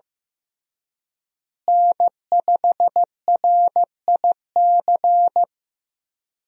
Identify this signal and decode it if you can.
Morse code, heard as a keyed tone, about 15 wpm, 700 Hz. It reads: N5RIC
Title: USA Call Signs - 5 Characters - Repeated 3x - 15wpm